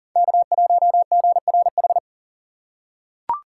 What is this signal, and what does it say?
Morse code, heard as a keyed tone, about 40 wpm, 700 Hz. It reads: K1GRH
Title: USA Call Signs - 5 Characters - 3x Character Spacing - 40wpm